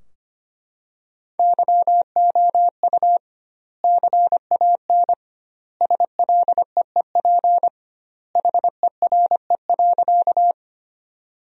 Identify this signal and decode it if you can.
Morse code, heard as a keyed tone, about 25 wpm, 700 Hz. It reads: YOU CAN SLEEP HERE.
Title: 2x Word-Spacing - Sentences from Top 500 Words - 25wpm